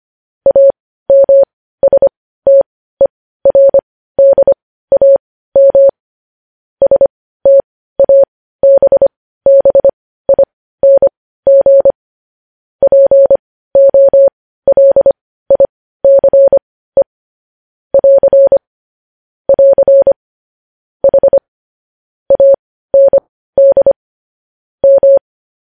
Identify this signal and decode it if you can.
Morse code, heard as a keyed tone, about 25 wpm, spaced as for 15 wpm, 550 Hz. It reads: AMSTERDAM STABBING POLICE <AR> <AR> H AND M
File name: podcast / News headlines-1x-25x15